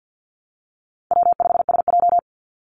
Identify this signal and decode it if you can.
Morse code, heard as a keyed tone, about 50 wpm, 700 Hz. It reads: W5SJ
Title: Call Signs Repeated 2x - 50wpm